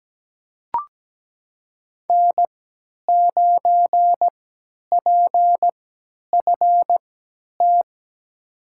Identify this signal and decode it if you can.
Morse code, heard as a keyed tone, about 17 wpm, spaced as for 10 wpm, 700 Hz. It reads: N9PFT
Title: USA Call Signs - 3x Character Spacing - 17wpm